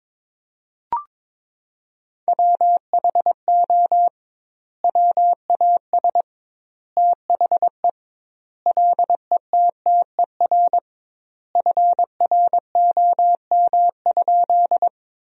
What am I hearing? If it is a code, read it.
Morse code, heard as a keyed tone, about 22 wpm, 700 Hz. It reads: WHO WAS THE LETTER FROM?
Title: 2x Word-Spacing - Sentences from Top 300 Words - 22wpm